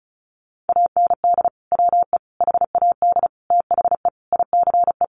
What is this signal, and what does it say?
Morse code, heard as a keyed tone, about 34 wpm, 700 Hz. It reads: AND WE HAD THE ICE
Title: Sentences from Top 1500 Words 35wpm